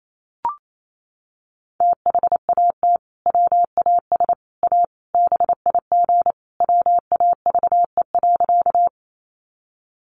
Morse code, heard as a keyed tone, about 28 wpm, 700 Hz. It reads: THAT WAS A BIG WAVE.
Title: Sentences from Top 1400 Words 28wpm